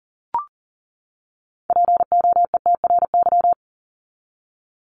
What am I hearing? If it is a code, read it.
Morse code, heard as a keyed tone, about 38 wpm, 700 Hz. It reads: POETRY
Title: Top 1751-2000 Words 40wpm